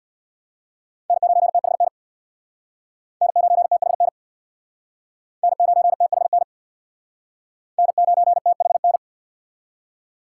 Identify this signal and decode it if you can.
Morse code, heard as a keyed tone, about 64 wpm, 700 Hz. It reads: N9THN N9THN N9THN N9THN